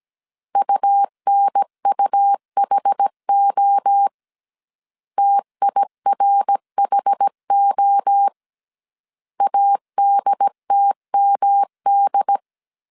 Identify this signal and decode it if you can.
Morse code, heard as a keyed tone, about 17 wpm, 800 Hz. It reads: UNUHO TIRHO ADTMD